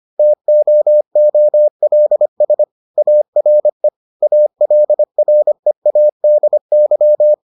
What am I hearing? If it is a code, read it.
Morse code, heard as a keyed tone, about 25 wpm, 600 Hz. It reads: TOOLS ARE ALREADY